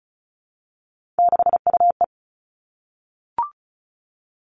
Morse code, heard as a keyed tone, about 34 wpm, 700 Hz. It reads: 6UE
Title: All Cut Numbers - 3-Digit 35wpm